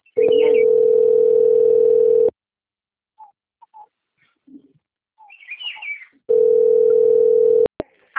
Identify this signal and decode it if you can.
Ringback tone (North America), 440 + 480 Hz.